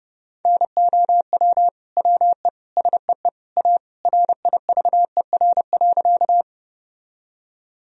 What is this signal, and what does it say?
Morse code, heard as a keyed tone, about 30 wpm, 700 Hz. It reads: NOW WE SEE A RIVER.
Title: Sentences from Top 1000 Words - Review Part 1 - 30wpm